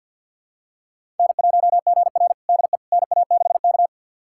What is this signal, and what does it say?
Morse code, heard as a keyed tone, about 49 wpm, 690 Hz. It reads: N1GR DE NA6K
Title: USA Callsign DE Callsign - 4 Characters - 50wpm